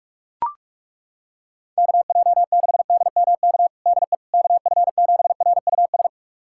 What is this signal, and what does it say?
Morse code, heard as a keyed tone, about 41 wpm, 700 Hz. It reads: KJ6DMK DE KF7RUS
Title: USA Callsign DE Callsign - 6 Characters - 45wpm